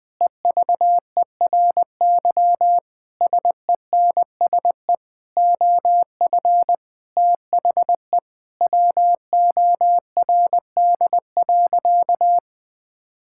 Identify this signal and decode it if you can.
Morse code, heard as a keyed tone, about 21 wpm, 700 Hz. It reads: EVERY SENSE OF THE WORD.
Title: Sentences with 6-Word Phrases - 20wpm